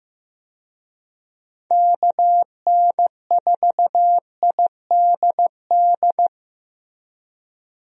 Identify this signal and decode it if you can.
Morse code, heard as a keyed tone, about 15 wpm, 700 Hz. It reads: KN4IDD
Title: USA Call Signs - 6 Characters - Repeated 3x - 15wpm